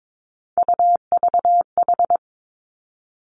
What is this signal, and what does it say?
Morse code, heard as a keyed tone, about 22 wpm, 700 Hz. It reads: UVH
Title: Three Challenging Letter-Number Combinations 22wpm